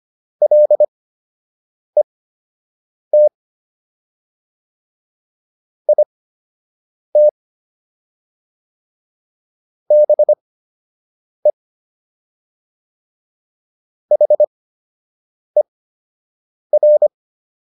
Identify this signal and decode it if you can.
Morse code, heard as a keyed tone, about 25 wpm, spaced as for 7 wpm, 600 Hz. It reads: LET IT BE HER